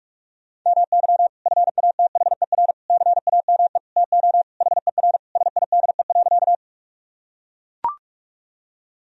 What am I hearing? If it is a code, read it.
Morse code, heard as a keyed tone, about 42 wpm, 700 Hz. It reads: MY FATHER CAME TO HER SIDE.
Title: Sentences from Top 1000 Words - Review Part 1 - 45wpm